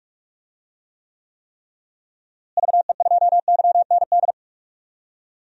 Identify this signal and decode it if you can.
Morse code, heard as a keyed tone, about 43 wpm, 700 Hz. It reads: VE2YND
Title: DXCC-100 Call Signs - 6 Characters - Repeated 3x - 45wpm